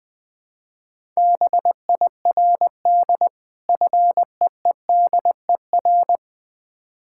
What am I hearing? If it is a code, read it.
Morse code, heard as a keyed tone, about 20 wpm, 690 Hz. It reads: BIRD FEEDER
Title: Difficult Words with Hints - 20wpm